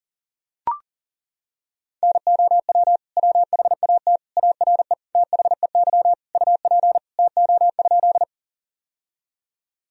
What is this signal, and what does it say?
Morse code, heard as a keyed tone, about 40 wpm, 700 Hz. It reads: NOW WHAT ARE THEY UP TO?
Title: Sentences from Top 1000 Words - Review Part 1 - 40wpm